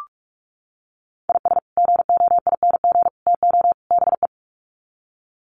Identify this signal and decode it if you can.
Morse code, heard as a keyed tone, about 44 wpm, 700 Hz. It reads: IS GOING TO BE
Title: Sets of 4 Words 45wpm